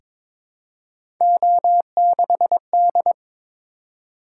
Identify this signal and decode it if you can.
Morse code, heard as a keyed tone, about 22 wpm, 700 Hz. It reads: O6D
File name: practice / 22wpm / Three Letter-Number Combinations - 22wpm